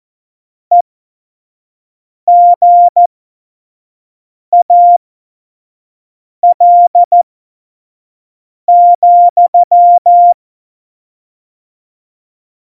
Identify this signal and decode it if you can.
Morse code, heard as a keyed tone, about 14 wpm, spaced as for 5 wpm, 700 Hz. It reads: EGAL,
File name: Morsekurs / Witze / Witz 03.Dezember 2019, 25 F70